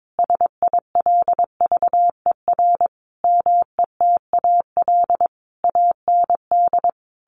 Morse code, heard as a keyed tone, about 22 wpm, 700 Hz. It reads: SILVER METAL AND
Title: A Sound Of Thunder 22wpm